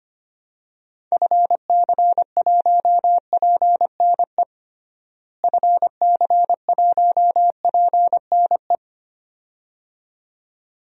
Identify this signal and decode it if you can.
Morse code, heard as a keyed tone, about 25 wpm, 700 Hz. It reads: FC1PNE FC1PNE
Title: DXCC-100 Call Signs - 6 Characters - Repeated 2x - 25wpm